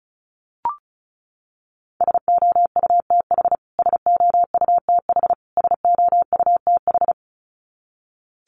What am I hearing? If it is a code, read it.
Morse code, heard as a keyed tone, about 34 wpm, 700 Hz. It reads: SOUTH SOUTH SOUTH
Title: Top 500 Words Repeated 35wpm